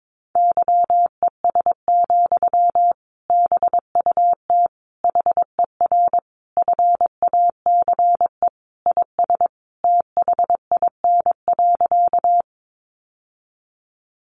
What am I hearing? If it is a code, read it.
Morse code, heard as a keyed tone, about 22 wpm, 700 Hz. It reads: YES, BUT HER FACE IS THIN.
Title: Sentences from Top 1200 Words 22wpm